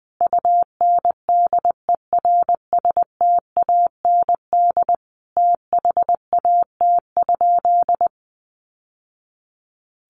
Morse code, heard as a keyed tone, about 20 wpm, 700 Hz. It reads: UNDERSTAND THAT?
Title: Sentences from Top 600 Words 20wpm